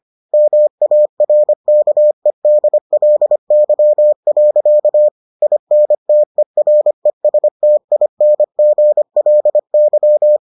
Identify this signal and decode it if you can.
Morse code, heard as a keyed tone, about 25 wpm, 600 Hz. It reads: MARKEDLY. INTERESTINGLY